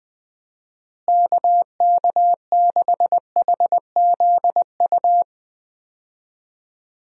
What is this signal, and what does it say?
Morse code, heard as a keyed tone, about 20 wpm, 700 Hz. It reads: KK6HZU
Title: USA Call Signs - 6 Characters - Repeated 3x - 20wpm